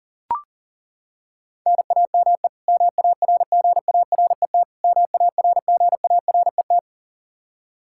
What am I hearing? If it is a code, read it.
Morse code, heard as a keyed tone, about 40 wpm, 700 Hz. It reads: NAME MARGARET MARGARET
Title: K1USN SST QSO Practice 40wpm